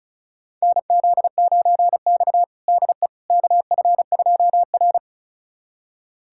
Morse code, heard as a keyed tone, about 36 wpm, 700 Hz. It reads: NZ9X DE KF2R
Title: USA Callsign DE Callsign - 4 Characters - 35wpm